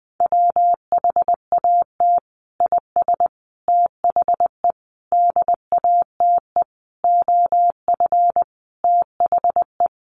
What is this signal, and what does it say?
Morse code, heard as a keyed tone, about 19 wpm, 700 Hz. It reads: WHAT IS THE DATE OF THE
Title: Sentences from Top 1100 Words 20wpm